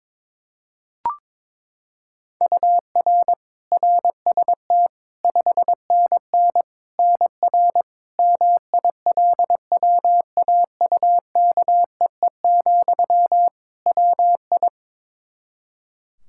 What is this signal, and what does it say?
Morse code, heard as a keyed tone, about 22 wpm, 700 Hz. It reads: UR RST 5NN NR MILWAUKEE, WI
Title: Practice QSO - Exchange 1 - Beginning - 22wpm